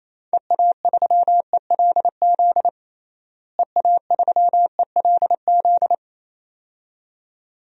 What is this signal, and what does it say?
Morse code, heard as a keyed tone, about 28 wpm, 700 Hz. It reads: EA3ELZ EA3ELZ
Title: DXCC-100 Call Signs - 6 Characters - Repeated 2x - 28wpm